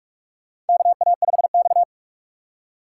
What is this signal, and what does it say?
Morse code, heard as a keyed tone, about 44 wpm, 700 Hz. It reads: KA5X
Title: USA Call Signs - 4 Characters - 2x Character Spacing - 45wpm